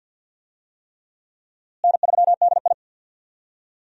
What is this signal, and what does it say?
Morse code, heard as a keyed tone, about 51 wpm, 700 Hz. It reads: N3DI
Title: USA Call Signs - 4 Characters - Repeated 3x - 50wpm